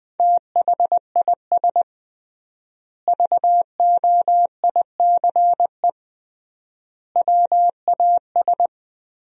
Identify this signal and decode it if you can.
Morse code, heard as a keyed tone, about 21 wpm, 700 Hz. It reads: THIS VOICE WAS
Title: 3x Word-Spacing - Sentences from Top 500 Words - 20wpm